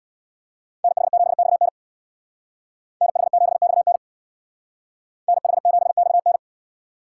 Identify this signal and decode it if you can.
Morse code, heard as a keyed tone, about 63 wpm, 700 Hz. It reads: NH7XN NH7XN NH7XN